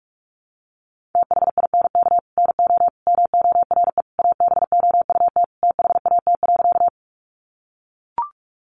Morse code, heard as a keyed tone, about 44 wpm, 700 Hz. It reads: THINK NO MORE ABOUT THAT.
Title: Sentences from Top 200 Words 45wpm